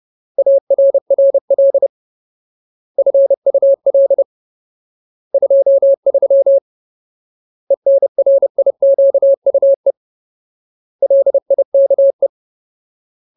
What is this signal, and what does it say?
Morse code, heard as a keyed tone, about 30 wpm, 550 Hz. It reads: ARRL FUL 23 ENRIQUE LIKE
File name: Ultimate Word List / 30WPM Ultimate